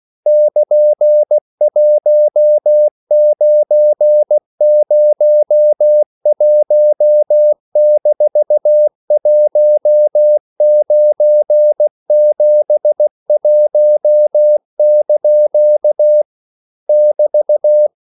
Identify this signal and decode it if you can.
Morse code, heard as a keyed tone, about 16 wpm, 600 Hz. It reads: <KN>1901-1971) <BT>
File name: CW / Practice / EnglishQuotations0416